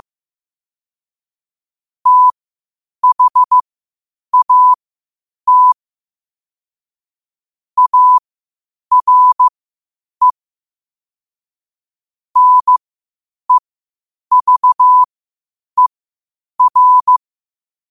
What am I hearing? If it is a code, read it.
Morse code, heard as a keyed tone, about 15 wpm, spaced as for 8 wpm, 1000 Hz. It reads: THAT ARE NEVER
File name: qotdfiles / 8wpm